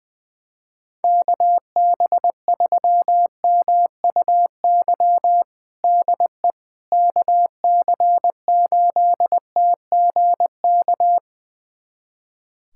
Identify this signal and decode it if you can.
Morse code, heard as a keyed tone, about 20 wpm, 700 Hz. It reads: KB3MUY DE KC8TGK
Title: USA Callsign DE Callsign - 6 Characters - 20wpm